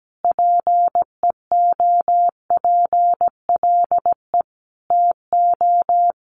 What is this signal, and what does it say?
Morse code, heard as a keyed tone, about 17 wpm, 700 Hz. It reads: PEOPLE TO
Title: Sentences from Top 1400 Words 17wpm